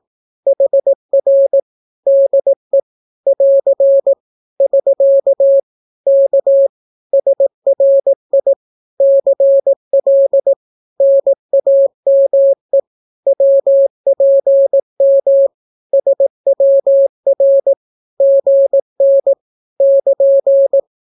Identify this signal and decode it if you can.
Morse code, heard as a keyed tone, about 18 wpm, 550 Hz. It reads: HR DE <AR> <SK> K SRI CL NAME WPM SWR GN <KN>
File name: naukapodstaw / skroty / skroty